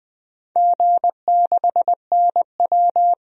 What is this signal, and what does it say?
Morse code, heard as a keyed tone, about 20 wpm, 700 Hz. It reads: G6NW
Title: DXCC-100 Call Signs - 4 Characters - 20wpm